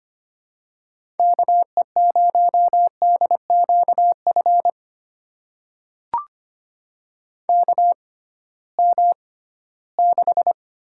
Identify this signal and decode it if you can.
Morse code, heard as a keyed tone, about 25 wpm, spaced as for 16 wpm, 700 Hz. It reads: KE0DQF KM6
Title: USA Call Signs - 6 Characters - 6x Character Spacing - 25wpm